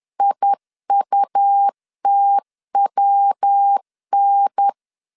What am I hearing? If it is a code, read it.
Morse code, heard as a keyed tone, about 10 wpm, 800 Hz. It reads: IUTWN
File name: LEZIONI / 6 PKX0 / 6 PK 10wpm 2436